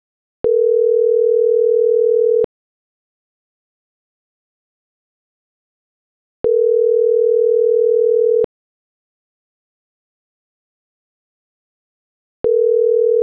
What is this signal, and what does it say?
Ringback tone (North America), 440 + 480 Hz.